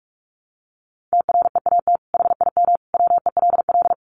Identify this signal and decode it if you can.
Morse code, heard as a keyed tone, about 44 wpm, 700 Hz. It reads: TREAT HIM WELL
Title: Sentences from Top 1300 Words 45wpm